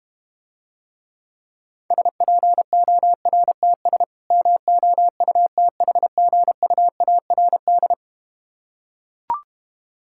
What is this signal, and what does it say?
Morse code, heard as a keyed tone, about 32 wpm, 700 Hz. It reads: SPORTS MOUTHGUARD